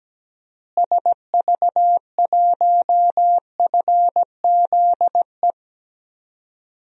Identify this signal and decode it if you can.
Morse code, heard as a keyed tone, about 17 wpm, 700 Hz. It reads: SV1FZE
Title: DXCC-100 Call Signs - 6 Characters - Repeated 3x - 17wpm